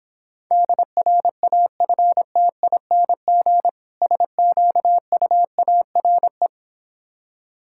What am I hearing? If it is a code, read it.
Morse code, heard as a keyed tone, about 26 wpm, 700 Hz. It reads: DRAFTING SQUARE